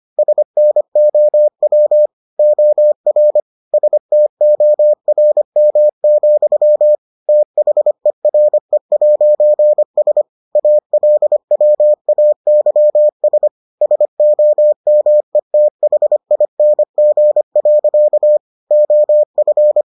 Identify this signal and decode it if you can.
Morse code, heard as a keyed tone, about 25 wpm, 600 Hz. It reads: SNOW OR STORM, THERE'S ALWAYS SOMETHING. OF